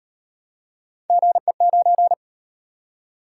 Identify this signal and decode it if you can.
Morse code, heard as a keyed tone, about 38 wpm, 700 Hz. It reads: GE9